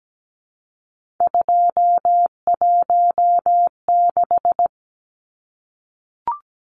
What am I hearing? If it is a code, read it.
Morse code, heard as a keyed tone, about 17 wpm, 700 Hz. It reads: 216
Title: Numbers - 3-Digit 17wpm